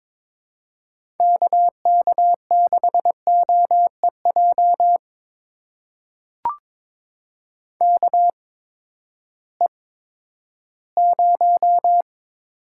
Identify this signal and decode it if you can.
Morse code, heard as a keyed tone, about 22 wpm, spaced as for 13 wpm, 700 Hz. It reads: KK6OEJ KE0
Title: USA Call Signs - 6 Characters - 8x Character Spacing - 22wpm